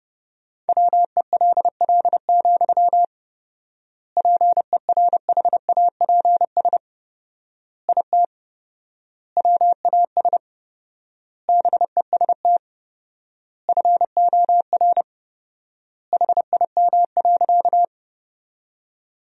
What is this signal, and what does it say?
Morse code, heard as a keyed tone, about 30 wpm, 700 Hz. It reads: WELL, PERHAPS IT WAS BEST FOR HIM.
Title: 4x Word-Spacing - Sentences from Top 500 Words - 30wpm